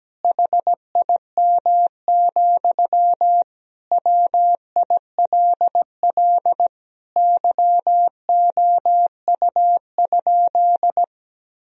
Morse code, heard as a keyed tone, about 17 wpm, 700 Hz. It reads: HIM, WILL YOU?
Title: Sentences from Top 1000 Words - Review Part 1 - 17wpm